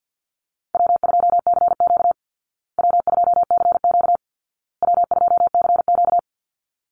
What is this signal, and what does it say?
Morse code, heard as a keyed tone, about 50 wpm, 700 Hz. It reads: W2CQ W2CQ W2CQ
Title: Easy Call Signs Repeated 3x - 50wpm